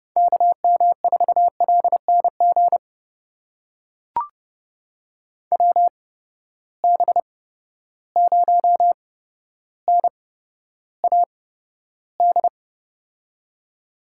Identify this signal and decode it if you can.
Morse code, heard as a keyed tone, about 30 wpm, spaced as for 13 wpm, 700 Hz. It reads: KM4LNG WB0NAD
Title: USA Call Signs - 6 Characters - 8x Character Spacing - 30wpm